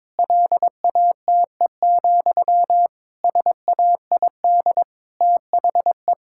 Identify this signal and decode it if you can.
Morse code, heard as a keyed tone, about 22 wpm, 700 Hz. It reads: LATE, SAID THE
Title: Sentences from Top 1000 Words - Review Part 6 - 22wpm